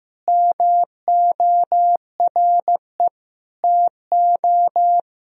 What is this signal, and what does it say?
Morse code, heard as a keyed tone, about 15 wpm, 700 Hz. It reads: MORE TO
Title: Sentences from Top 1000 Words - Review Part 1 - 15wpm